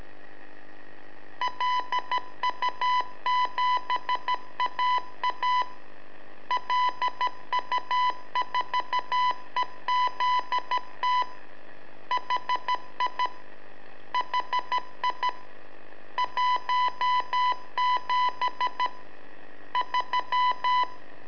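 Morse code, heard as a keyed tone, about 13 wpm, 1000 Hz. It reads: LU7AA LU4EZT HI HI 17 3
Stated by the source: Telemetry in Morse from LU7AA and LU4EZ, AMSAT and Turma Mar del Plata, sent from a balloon.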